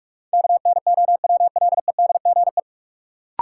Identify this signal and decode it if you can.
Morse code, heard as a keyed tone, about 46 wpm, 700 Hz. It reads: KNOWLEDGE
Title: Sentences from Top 600 Words - New Word Drill - 45wpm